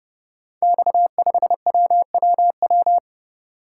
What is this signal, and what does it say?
Morse code, heard as a keyed tone, about 29 wpm, 700 Hz. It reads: X5WWW
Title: Code Groups - 5 Letter:Numbers - 30wpm